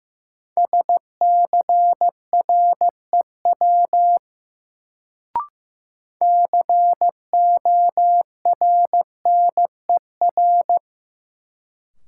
Morse code, heard as a keyed tone, about 15 wpm, 700 Hz. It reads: SCREW CORNER
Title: Sentences from Top 1600 Words - New Word Drill - 15wpm